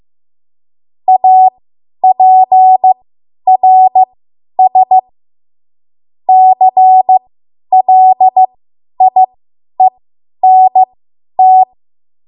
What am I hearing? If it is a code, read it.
Morse code, heard as a keyed tone, about 15 wpm, spaced as for 10 wpm, 750 Hz. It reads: APRS CLIENT
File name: Morse / 10WPM